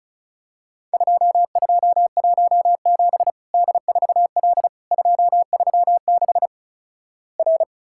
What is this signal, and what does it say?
Morse code, heard as a keyed tone, about 36 wpm, 700 Hz. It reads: 2217 D4L 236 R
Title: Worked All Europe DX Contest - QTC Exchange - 35wpm